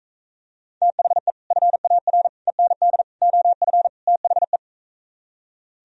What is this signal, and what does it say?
Morse code, heard as a keyed tone, about 45 wpm, 700 Hz. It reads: THE FAR END OF THE